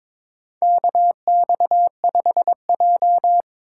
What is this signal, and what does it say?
Morse code, heard as a keyed tone, about 22 wpm, 700 Hz. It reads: KX5J
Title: USA Call Signs - 4 Characters - 6x Character Spacing - 22wpm